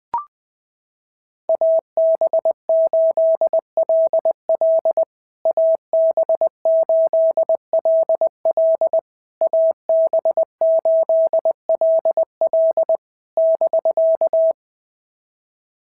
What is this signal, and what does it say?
Morse code, heard as a keyed tone, about 20 wpm, 650 Hz. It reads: AB8LL AB8LL AB8LL <BK>
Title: POTA QSO Practice - 20wpm